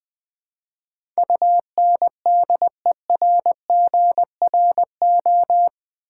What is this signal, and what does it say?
Morse code, heard as a keyed tone, about 20 wpm, 700 Hz. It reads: UNDERGRO
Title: Build A Word - 4 Letters At A Time - 11-Letter Words - 20wpm